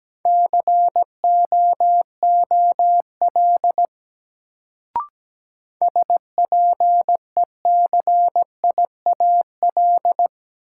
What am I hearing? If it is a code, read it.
Morse code, heard as a keyed tone, about 17 wpm, 700 Hz. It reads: COOL SPECIAL
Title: Sentences from Top 1200 Words - New Word Drill Alt - 17wpm